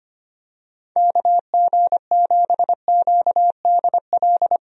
Morse code, heard as a keyed tone, about 25 wpm, 700 Hz. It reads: KG7QDL
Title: USA Call Signs - 6 Characters - 25wpm